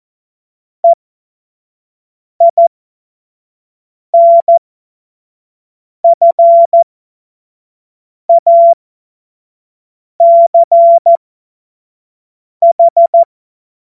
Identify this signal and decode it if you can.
Morse code, heard as a keyed tone, about 14 wpm, spaced as for 5 wpm, 680 Hz. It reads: EINFACH
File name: Witz-09-September-2025-25-F70